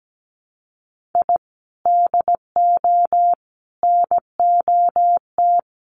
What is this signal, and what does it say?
Morse code, heard as a keyed tone, about 17 wpm, 700 Hz. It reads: I DO NOT
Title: Sets of 5 Words Encore 17wpm